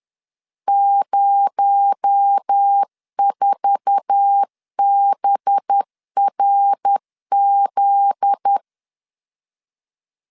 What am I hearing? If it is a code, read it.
Morse code, heard as a keyed tone, about 10 wpm, 800 Hz. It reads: 04BRZ
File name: LEZIONI / 8 YJ45 / 8 YJ4 10wpm 2633